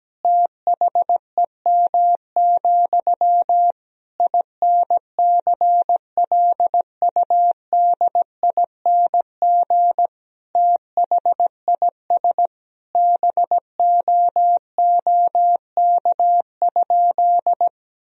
Morse code, heard as a keyed tone, about 17 wpm, 700 Hz. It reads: THEM, INCLUDING THIS BOOK?
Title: Sentences from Top 1000 Words - Review Part 4 - 17wpm